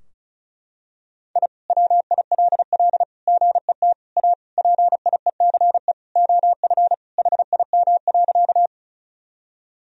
Morse code, heard as a keyed tone, about 34 wpm, 700 Hz. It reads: I WILL GET A PIECE OF HIM.
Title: Sentences from Top 1000 Words - Review Part 2 - 35wpm